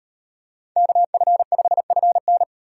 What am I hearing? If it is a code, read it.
Morse code, heard as a keyed tone, about 36 wpm, 700 Hz. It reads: KF5FN